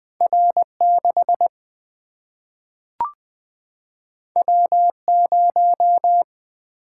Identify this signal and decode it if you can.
Morse code, heard as a keyed tone, about 20 wpm, 700 Hz. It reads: R6 W0
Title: Single-Letter Single-Number 20wpm